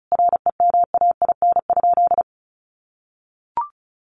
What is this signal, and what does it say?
Morse code, heard as a keyed tone, about 34 wpm, 700 Hz. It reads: REMAIN?
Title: Sentences from Top 1200 Words 35wpm